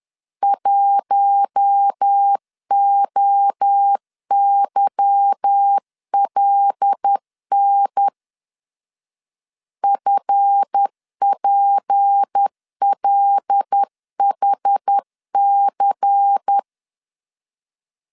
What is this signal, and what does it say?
Morse code, heard as a keyed tone, about 10 wpm, 800 Hz. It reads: 1OYLN FPLHC
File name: LEZIONI / 8 YJ45 / 8 Y 10wpm 2615